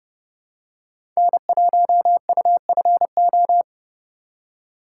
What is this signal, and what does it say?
Morse code, heard as a keyed tone, about 30 wpm, 700 Hz. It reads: N1UFO
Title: USA Call Signs - 5 Characters - 4x Character Spacing - 30wpm